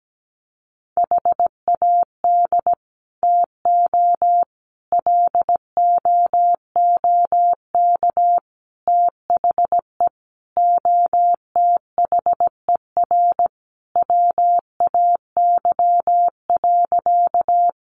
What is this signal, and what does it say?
Morse code, heard as a keyed tone, about 17 wpm, 700 Hz. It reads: HAD TO LOOK THE OTHER WAY.
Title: Sentences from Top 100 Words 17wpm